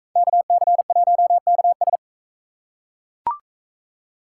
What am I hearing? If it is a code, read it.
Morse code, heard as a keyed tone, about 40 wpm, 700 Hz. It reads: KC1KS